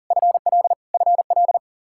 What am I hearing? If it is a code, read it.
Morse code, heard as a keyed tone, about 40 wpm, 700 Hz. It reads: FL FL
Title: Practice QSO - Exchange 1 - Beginning - 40wpm